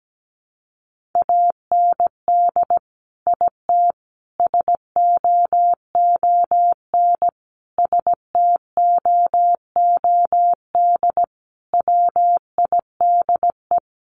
Morse code, heard as a keyed tone, about 17 wpm, 700 Hz. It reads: AND IT SOON STOOD WIDE
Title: Sentences from Top 900 Words 17wpm